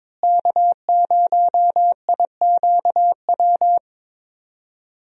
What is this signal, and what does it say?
Morse code, heard as a keyed tone, about 23 wpm, 700 Hz. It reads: K0IQW
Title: USA Call Signs - 5 Characters - Repeated 2x - 22wpm